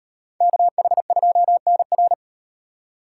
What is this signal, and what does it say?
Morse code, heard as a keyed tone, about 38 wpm, 700 Hz. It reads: KH2NR